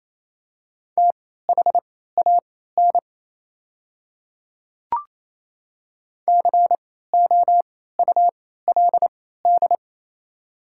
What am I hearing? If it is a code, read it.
Morse code, heard as a keyed tone, about 28 wpm, spaced as for 16 wpm, 700 Hz. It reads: THAN COULD
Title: 3x Character Spacing - Top 100 Words - 28wpm